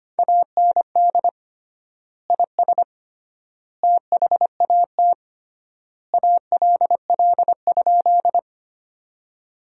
Morse code, heard as a keyed tone, about 25 wpm, 700 Hz. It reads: AND IS THAT ALL?
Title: 3x Word-Spacing - Sentences from Top 100 Words - 25wpm